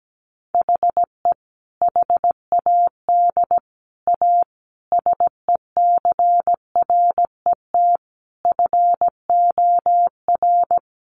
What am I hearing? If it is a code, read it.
Morse code, heard as a keyed tone, about 17 wpm, 700 Hz. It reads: HE HAD A SECRET FOR
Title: Sentences from Top 1200 Words 17wpm